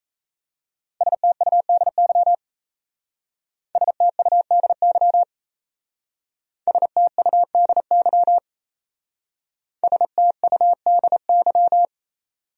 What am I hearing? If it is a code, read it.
Morse code, heard as a keyed tone, about 35 wpm, 700 Hz. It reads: STUDY STUDY STUDY STUDY
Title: Speed-Racing Top 500 Words 28wpm